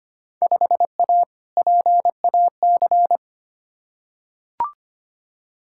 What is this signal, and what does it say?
Morse code, heard as a keyed tone, about 25 wpm, 700 Hz. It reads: 5A PAC
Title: Field Day Exchanges Encore - 25wpm